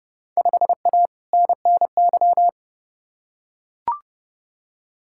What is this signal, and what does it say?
Morse code, heard as a keyed tone, about 32 wpm, 700 Hz. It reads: 5A NNY
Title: Field Day Exchanges - 30wpm